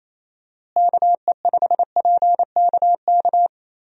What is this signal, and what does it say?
Morse code, heard as a keyed tone, about 28 wpm, 700 Hz. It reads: KE5PKK
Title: USA Call Signs - 6 Characters - Repeated 2x - 28wpm